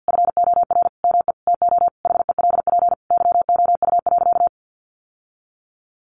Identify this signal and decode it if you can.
Morse code, heard as a keyed tone, about 50 wpm, 700 Hz. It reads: FOR ME TO HELP YOU.
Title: Sentences from Top 1200 Words 50wpm